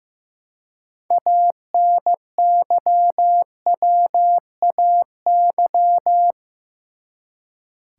Morse code, heard as a keyed tone, about 15 wpm, 700 Hz. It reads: ANYWAY
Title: Six Letter Words - 15wpm